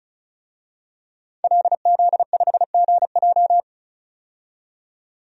Morse code, heard as a keyed tone, about 34 wpm, 700 Hz. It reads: LZ5GJ
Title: DXCC-100 Call Signs - 5 Characters - 35wpm